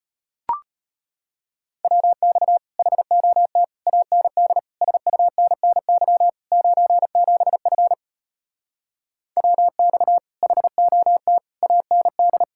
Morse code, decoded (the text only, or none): WX HOT AND SUNNY 97F WX HOT AND